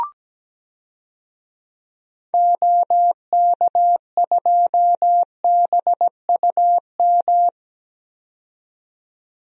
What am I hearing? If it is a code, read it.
Morse code, heard as a keyed tone, about 17 wpm, 700 Hz. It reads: OK2BUM
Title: DXCC-100 Call Signs - 6 Characters - 17wpm